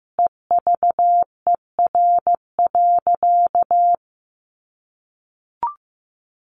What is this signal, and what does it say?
Morse code, heard as a keyed tone, about 15 wpm, 700 Hz. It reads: EVER.
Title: Sentences from Top 1400 Words 15wpm